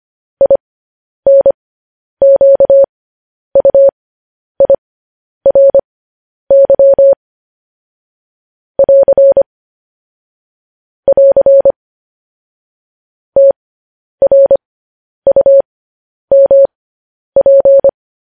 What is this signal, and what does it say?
Morse code, heard as a keyed tone, about 25 wpm, spaced as for 10 wpm, 550 Hz. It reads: INQUIRY <AR> <AR> TRUMP
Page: news headlines in morse code at 10 wpm